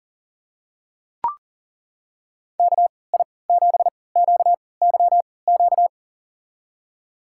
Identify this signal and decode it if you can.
Morse code, heard as a keyed tone, about 39 wpm, spaced as for 23 wpm, 700 Hz. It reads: KI7QYQ
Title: USA Call Signs - 3x Character Spacing - 40wpm